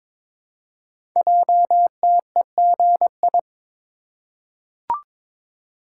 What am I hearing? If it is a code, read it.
Morse code, heard as a keyed tone, about 22 wpm, 700 Hz. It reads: JTEGI
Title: Code Groups - 5 Letters - 22wpm